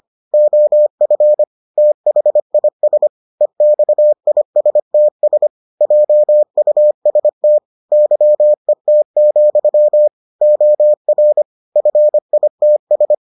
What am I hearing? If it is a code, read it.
Morse code, heard as a keyed tone, about 25 wpm, 600 Hz. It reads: OF THIS EXISTS JUST YET, OR FITS